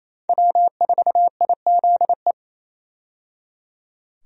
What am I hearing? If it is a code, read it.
Morse code, heard as a keyed tone, about 28 wpm, 700 Hz. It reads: W4IZE
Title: USA Call Signs - 5 Characters - 28wpm